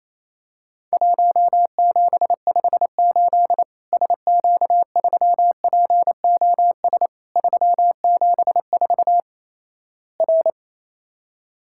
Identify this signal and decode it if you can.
Morse code, heard as a keyed tone, about 28 wpm, 700 Hz. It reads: 1758 SQ3POS 374 R
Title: Worked All Europe DX Contest - QTC Exchange - 28wpm